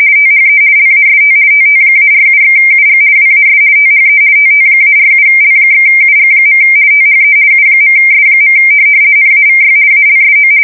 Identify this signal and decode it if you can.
RTTY, about 50 baud. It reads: THE LAST TT-47/UG WAS THE TT-47L/UG. THE U.S. NAVY ALSO
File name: fsk170-50B / teletype-corp